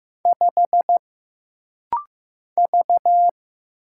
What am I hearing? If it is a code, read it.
Morse code, heard as a keyed tone, about 15 wpm, 700 Hz. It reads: 5 V
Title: Single Letter-Number Alt - Rapid-Fire 15wpm